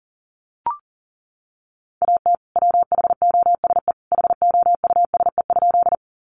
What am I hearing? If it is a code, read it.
Morse code, heard as a keyed tone, about 40 wpm, 700 Hz. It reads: AT WHOSE HOUSE?
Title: Sentences from Top 1200 Words 40wpm